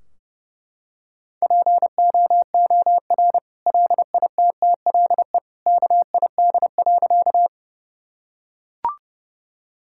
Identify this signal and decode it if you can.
Morse code, heard as a keyed tone, about 30 wpm, 700 Hz. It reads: POOR LITTLE KID.
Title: Sentences from Top 1600 Words 30wpm